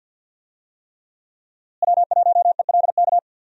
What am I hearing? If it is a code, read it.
Morse code, heard as a keyed tone, about 51 wpm, 700 Hz. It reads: W1ELK
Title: USA Call Signs - 5 Characters - Repeated 3x - 50wpm